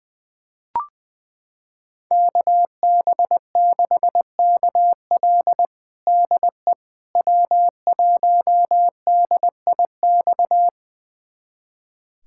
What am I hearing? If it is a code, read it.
Morse code, heard as a keyed tone, about 20 wpm, 700 Hz. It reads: KB6KL DE W1DIX
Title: USA Callsign DE Callsign 20wpm